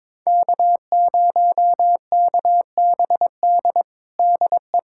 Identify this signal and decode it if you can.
Morse code, heard as a keyed tone, about 22 wpm, 700 Hz. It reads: K0KBD DE
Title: USA Callsign DE Callsign - 5 Characters - 22wpm